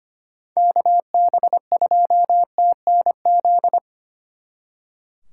Morse code, heard as a keyed tone, about 30 wpm, 700 Hz. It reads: KB2TNZ
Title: Speed Racing - USA 2x3 Call Signs 25wpm